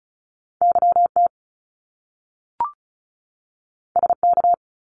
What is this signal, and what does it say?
Morse code, heard as a keyed tone, about 36 wpm, 700 Hz. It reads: YT SK
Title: Canadian Provinces and Territories 35wpm